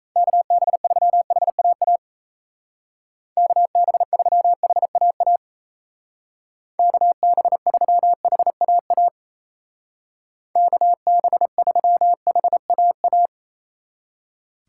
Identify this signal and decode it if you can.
Morse code, heard as a keyed tone, about 36 wpm, 700 Hz. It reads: KB3HAA KB3HAA KB3HAA KB3HAA